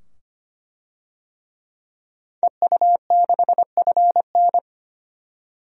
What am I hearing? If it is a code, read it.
Morse code, heard as a keyed tone, about 25 wpm, 700 Hz. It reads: EU6FN
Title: DXCC-100 Call Signs - 5 Characters - 25wpm